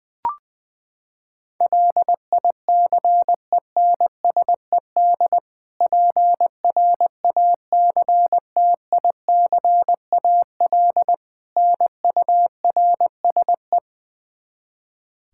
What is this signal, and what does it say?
Morse code, heard as a keyed tone, about 20 wpm, 700 Hz. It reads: LICENSED PRACTICAL NURSE
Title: Practice QSO - Exchange 3 - More Info - 20wpm